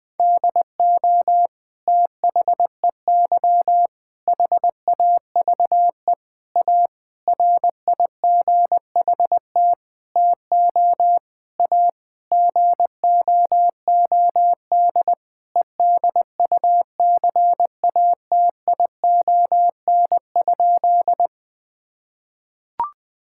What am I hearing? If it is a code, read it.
Morse code, heard as a keyed tone, about 21 wpm, 700 Hz. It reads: DO THEY HAVE A RIGHT TO A GOOD EDUCATION?
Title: Sentences from Top 1000 Words - Review Part 6 - 20wpm